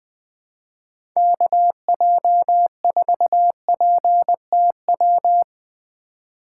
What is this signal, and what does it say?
Morse code, heard as a keyed tone, about 20 wpm, 700 Hz. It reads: KJ4PTW
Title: USA Call Signs - 6 Characters - 4x Character Spacing - 20wpm